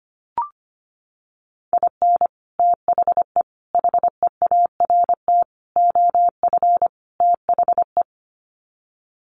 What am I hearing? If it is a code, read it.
Morse code, heard as a keyed tone, about 25 wpm, 700 Hz. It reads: IN THE HEART OF THE
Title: Sets of 5 Words 25wpm